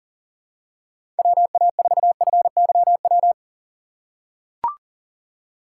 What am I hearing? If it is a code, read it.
Morse code, heard as a keyed tone, about 40 wpm, 700 Hz. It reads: WA4FYW
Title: USA Call Signs - 6 Characters - 3x Character Spacing - 40wpm